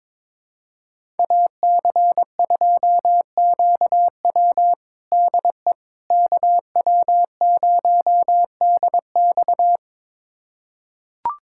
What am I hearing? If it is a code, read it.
Morse code, heard as a keyed tone, about 22 wpm, 700 Hz. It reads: AC2QW DE KW0DX
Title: USA Callsign DE Callsign - 5 Characters - 22wpm